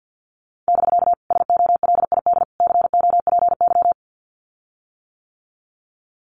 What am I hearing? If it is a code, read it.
Morse code, heard as a keyed tone, about 50 wpm, 700 Hz. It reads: <BK> SOLID COPY
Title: Common QSO Phrases Encore 50wpm